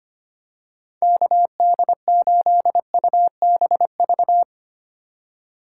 Morse code, heard as a keyed tone, about 25 wpm, 700 Hz. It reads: KD8UBV
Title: USA Call Signs - 2x Character Spacing - 25wpm